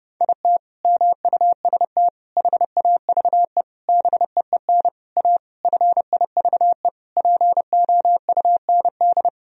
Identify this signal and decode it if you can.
Morse code, heard as a keyed tone, about 30 wpm, 700 Hz. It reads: IT MUST HAVE BEEN A FIVE POUND
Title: Sentences from Top 1000 Words - Review Part 2 - 30wpm